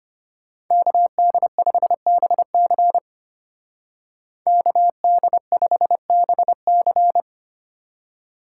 Morse code, heard as a keyed tone, about 33 wpm, 700 Hz. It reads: KD5BC KD5BC